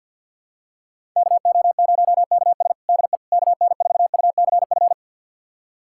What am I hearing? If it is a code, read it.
Morse code, heard as a keyed tone, about 51 wpm, 700 Hz. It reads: KO0KS DE KN4UCF
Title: USA Callsign DE Callsign 50wpm